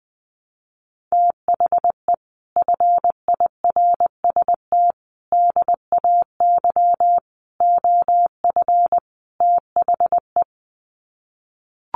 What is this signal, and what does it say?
Morse code, heard as a keyed tone, about 20 wpm, 700 Hz. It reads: THE FIRST DAY OF THE
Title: Sets of 5 Words Encore2 20wpm